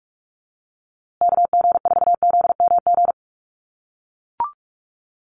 Morse code, heard as a keyed tone, about 44 wpm, 690 Hz. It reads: KG4ZMG
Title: USA Call Signs Repeated 2x 45wpm